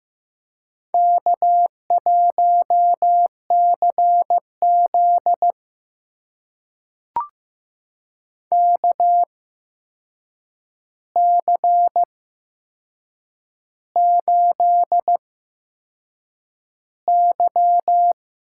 Morse code, heard as a keyed tone, about 15 wpm, spaced as for 6 wpm, 700 Hz. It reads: K1CZ KC8Y
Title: USA Call Signs - 8x Character Spacing - 15wpm